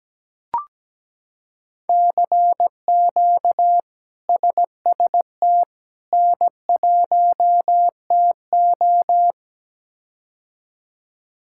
Morse code, heard as a keyed tone, about 17 wpm, 700 Hz. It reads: CQ SST N1TO
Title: K1USN SST QSO Practice - 17wpm